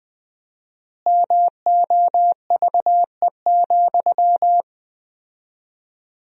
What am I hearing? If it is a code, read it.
Morse code, heard as a keyed tone, about 20 wpm, 700 Hz. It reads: MOVE,
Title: 4x Word-Spacing - Sentences from Top 300 Words - 20wpm